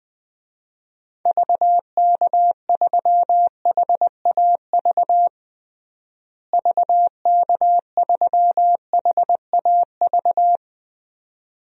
Morse code, heard as a keyed tone, about 21 wpm, 690 Hz. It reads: VK3HAV VK3HAV
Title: DXCC-100 Call Signs - 6 Characters - Repeated 2x - 20wpm